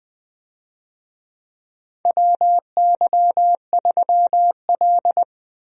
Morse code, heard as a keyed tone, about 20 wpm, 700 Hz. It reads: WY3L
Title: USA Call Signs - 4 Characters - Repeated 3x - 20wpm